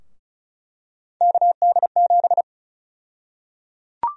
Morse code, heard as a keyed tone, about 34 wpm, 700 Hz. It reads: KD7
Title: USA Call Sign Prefixes - Repeated 2x - 35wpm